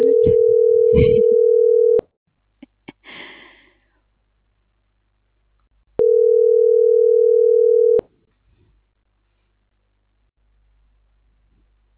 Ringback tone (North America), 440 + 480 Hz.